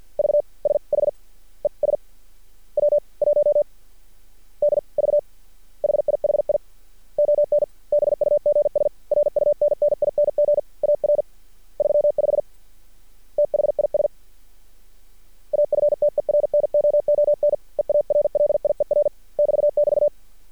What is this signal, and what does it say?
Morse code, heard as a keyed tone, about 50 wpm, 590 Hz. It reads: V S H E S W 1 D V HIHI ON BUGS RUNNING AR 35 THIS AFTERNOON EARLIER <BT><BT>
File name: archived / CRSnet